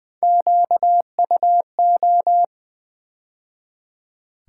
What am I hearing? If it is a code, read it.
Morse code, heard as a keyed tone, about 20 wpm, 700 Hz. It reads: QUO
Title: Build A Word - 3 Letters At A Time - 9-Letter Words - 20wpm